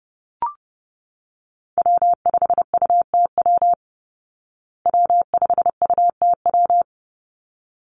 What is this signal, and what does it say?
Morse code, heard as a keyed tone, about 30 wpm, 700 Hz. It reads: W5UTW W5UTW
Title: Call Signs Repeated 3x - 30wpm